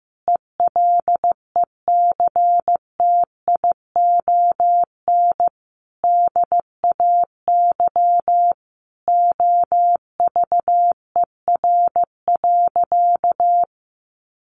Morse code, heard as a keyed tone, about 15 wpm, 700 Hz. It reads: ELECTION DAY OVER.
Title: A Sound Of Thunder 15wpm